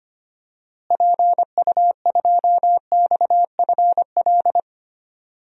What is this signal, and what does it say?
Morse code, heard as a keyed tone, about 25 wpm, 700 Hz. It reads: PU2XFL
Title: DXCC-100 Call Signs - 6 Characters - Repeated 2x - 25wpm